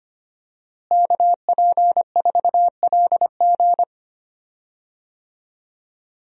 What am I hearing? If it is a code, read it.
Morse code, heard as a keyed tone, about 25 wpm, 700 Hz. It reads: KP4LG
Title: DXCC-100 Call Signs - 5 Characters - Repeated 2x - 25wpm